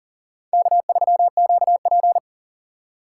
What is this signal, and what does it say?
Morse code, heard as a keyed tone, about 51 wpm, 700 Hz. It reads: K3QP